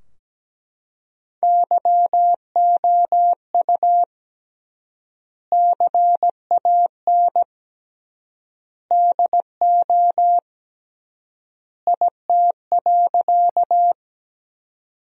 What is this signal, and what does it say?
Morse code, heard as a keyed tone, about 17 wpm, 700 Hz. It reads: YOU CAN DO IT.
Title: 3x Word-Spacing - Sentences from Top 100 Words - 17wpm